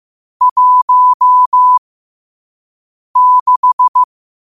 Morse code, heard as a keyed tone, about 15 wpm, spaced as for 5 wpm, 1000 Hz. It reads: 16
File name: qotdfiles / 5wpm